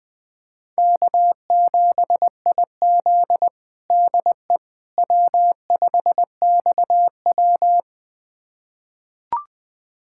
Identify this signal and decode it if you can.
Morse code, heard as a keyed tone, about 20 wpm, 700 Hz. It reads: K7IZ DE W5XW
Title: USA Callsign DE Callsign - 4 Characters - 20wpm